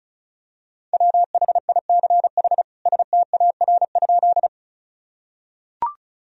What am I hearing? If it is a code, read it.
Morse code, heard as a keyed tone, about 34 wpm, 700 Hz. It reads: WHICH STAR?
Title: Sentences from Top 1000 Words - Review Part 2 - 35wpm